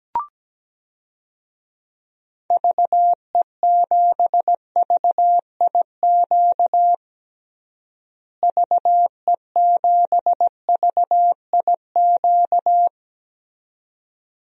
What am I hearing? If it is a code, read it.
Morse code, heard as a keyed tone, about 17 wpm, 700 Hz. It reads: VE7VIQ VE7VIQ
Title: DXCC-100 Call Signs - 6 Characters - Repeated 2x - 17wpm